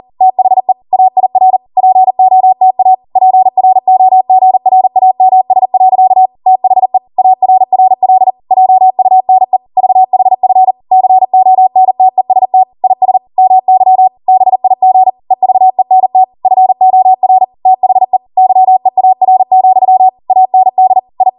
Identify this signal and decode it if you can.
Morse code, heard as a keyed tone, about 40 wpm, 750 Hz. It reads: THE AIR POTA PROGRAMS. THE ARRL JUNE VHF CONTEST IS MY BIG EVENT FOR THE YEAR, AND I